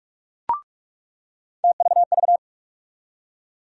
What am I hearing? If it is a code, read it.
Morse code, heard as a keyed tone, about 46 wpm, 700 Hz. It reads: TVV
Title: USA Call Sign Postfixes - 3 Letters 45wpm